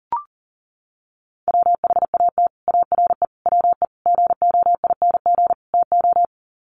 Morse code, heard as a keyed tone, about 40 wpm, 700 Hz. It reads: WHAT ARE WE GOING TO
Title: Sets of 5 Words 40wpm